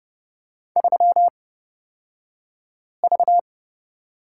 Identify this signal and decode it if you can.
Morse code, heard as a keyed tone, about 30 wpm, spaced as for 5 wpm, 700 Hz. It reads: 3V
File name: Lesson 38d - Letter q - Two Characters 30wpm